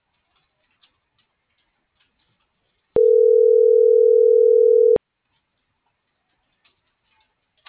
Ringback tone (North America), 440 + 480 Hz.